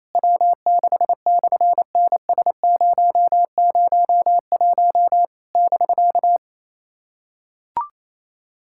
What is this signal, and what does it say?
Morse code, heard as a keyed tone, about 27 wpm, 700 Hz. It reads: W6/NS001 <BK>
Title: SOTA QSO Practice 28wpm